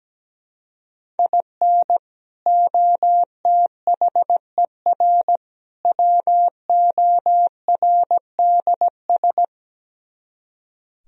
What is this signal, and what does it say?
Morse code, heard as a keyed tone, about 17 wpm, 700 Hz. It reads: IN OTHER WORDS
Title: Speed-Racing - Sets of 3 Words 17wpm